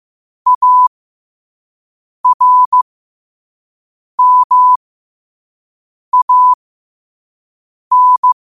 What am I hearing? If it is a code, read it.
Morse code, heard as a keyed tone, about 15 wpm, spaced as for 5 wpm, 1000 Hz. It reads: ARMAN
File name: qotdfiles / 5wpm